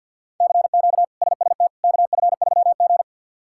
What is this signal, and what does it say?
Morse code, heard as a keyed tone, about 51 wpm, 700 Hz. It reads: CQ SST KF3G
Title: K1USN SST QSO Practice - 50wpm